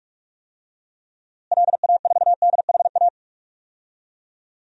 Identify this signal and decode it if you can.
Morse code, heard as a keyed tone, about 46 wpm, 700 Hz. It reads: LA4DHA
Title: DXCC-100 Call Signs - 6 Characters - Repeated 3x - 45wpm